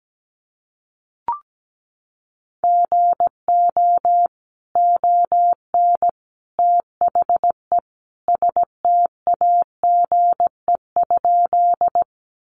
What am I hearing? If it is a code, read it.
Morse code, heard as a keyed tone, about 17 wpm, 700 Hz. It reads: GO ON THE STAGE?
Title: Sentences from Top 1200 Words 17wpm